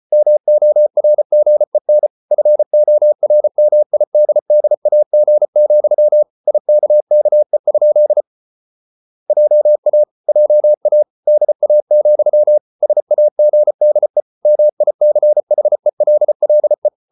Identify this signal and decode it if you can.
Morse code, heard as a keyed tone, about 34 wpm, 600 Hz. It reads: MORGEN FORMIDDAG, IKKE? JA JA DA, SAGDE MICHELLE
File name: Ferie QRQ 34wpm